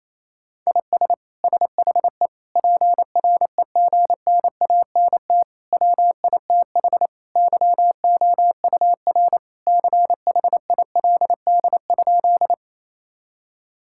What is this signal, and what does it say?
Morse code, heard as a keyed tone, about 27 wpm, 700 Hz. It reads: IS SHE PREGNANT WITH YOUR CHILD?
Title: Sentences from Top 1000 Words - Review Part 6 - 28wpm